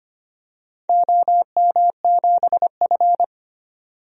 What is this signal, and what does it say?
Morse code, heard as a keyed tone, about 25 wpm, 700 Hz. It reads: OM7F
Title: DXCC-100 Call Signs - 4 Characters - 25wpm